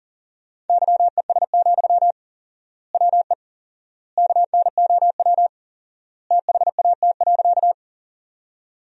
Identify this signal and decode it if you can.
Morse code, heard as a keyed tone, about 40 wpm, 700 Hz. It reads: YES, WE KNOW THAT.
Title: 4x Word-Spacing - Sentences from Top 500 Words - 40wpm